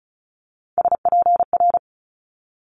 Morse code, heard as a keyed tone, about 36 wpm, 700 Hz. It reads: SPR
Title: Multi-letter Phonemes Quick Fire 35wpm